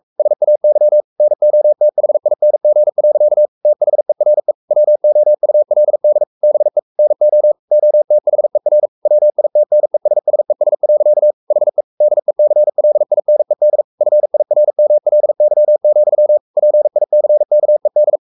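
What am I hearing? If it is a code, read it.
Morse code, heard as a keyed tone, about 44 wpm, 600 Hz. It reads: SAY NOTHING. THERE WOULD BE NO OTHER WITNESSES. HE DECLINED FIRMLY, PICKED
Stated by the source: Morse code at 43 wpm sending Ken Follett's novel Never in English, from CW med Gnister.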